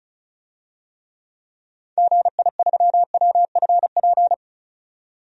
Morse code, heard as a keyed tone, about 36 wpm, 700 Hz. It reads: GI3WFP
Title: DXCC-100 Call Signs - 6 Characters - 35wpm